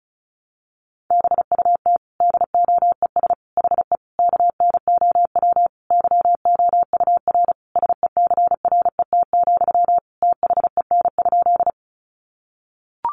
Morse code, heard as a keyed tone, about 34 wpm, 700 Hz. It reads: BUT DOES HE KNOW YOUR SECRET, THEN?
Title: Sentences from Top 1200 Words 35wpm